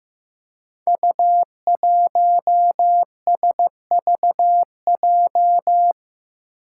Morse code, heard as a keyed tone, about 15 wpm, 700 Hz. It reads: U1SVJ
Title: Five Challenging Letter-Number Combinations 15wpm